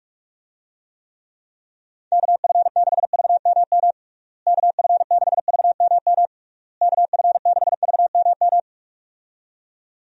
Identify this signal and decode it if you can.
Morse code, heard as a keyed tone, about 41 wpm, 700 Hz. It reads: KF6VMM KF6VMM KF6VMM
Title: USA Call Signs - 6 Characters - Repeated 3x - 45wpm